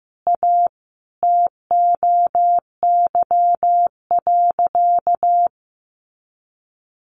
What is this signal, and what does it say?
Morse code, heard as a keyed tone, about 15 wpm, 700 Hz. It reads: A TOY.
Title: Easy Phrases 15wpm